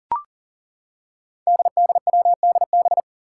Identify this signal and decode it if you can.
Morse code, heard as a keyed tone, about 41 wpm, 700 Hz. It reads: DDWDB
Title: Five Challenging Letter-Number Combinations 40wpm